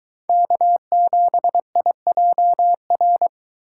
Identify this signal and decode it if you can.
Morse code, heard as a keyed tone, about 23 wpm, 700 Hz. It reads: K7IJR